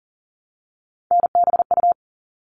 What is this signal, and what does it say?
Morse code, heard as a keyed tone, about 40 wpm, 700 Hz. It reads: NBU
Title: All Cut Numbers - 3-Digit 40wpm